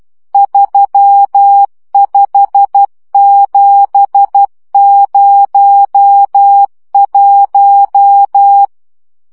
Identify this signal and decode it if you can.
Morse code, heard as a keyed tone, about 12 wpm, 800 Hz. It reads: 35701